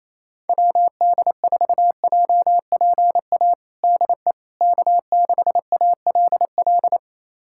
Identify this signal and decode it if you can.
Morse code, heard as a keyed tone, about 28 wpm, 700 Hz. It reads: WD4JPA DE K6ALL
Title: USA Callsign DE Callsign 28wpm